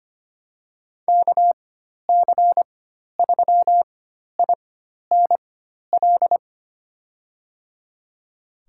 Morse code, heard as a keyed tone, about 25 wpm, spaced as for 12 wpm, 700 Hz. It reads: KC3INL
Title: USA Call Signs - 4x Character Spacing - 25wpm